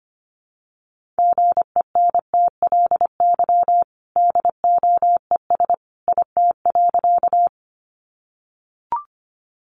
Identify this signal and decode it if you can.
Morse code, heard as a keyed tone, about 25 wpm, 700 Hz. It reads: GENTLY DOES IT.
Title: Sentences from Top 1500 Words 25wpm